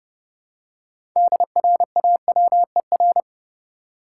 Morse code, heard as a keyed tone, about 32 wpm, 700 Hz. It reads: DRAWER
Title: Six Letter Words - 30wpm